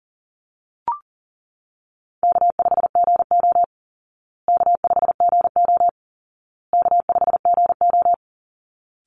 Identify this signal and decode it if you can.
Morse code, heard as a keyed tone, about 40 wpm, 700 Hz. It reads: K5GO K5GO K5GO
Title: Easy Call Signs Repeated 3x - 40wpm